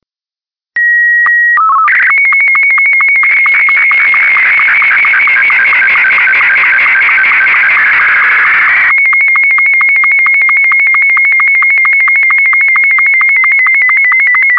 SSTV (slow-scan television), Martin 2.
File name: sstvmartin2